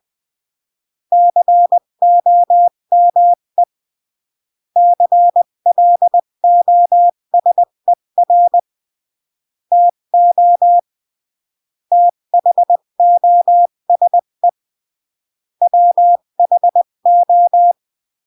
Morse code, heard as a keyed tone, about 20 wpm, 700 Hz. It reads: COME CLOSER TO THOSE WHO
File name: CW / jodorowsky / 82 commandments 20-15